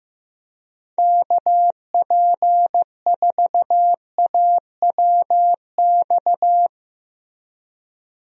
Morse code, heard as a keyed tone, about 15 wpm, 700 Hz. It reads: KP4AWX
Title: DXCC-100 Call Signs - 6 Characters - Repeated 3x - 15wpm